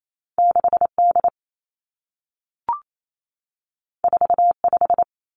Morse code, heard as a keyed tone, about 28 wpm, 700 Hz. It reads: 6 D 4 5
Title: Two Challenging Letter-Number Combinations 28wpm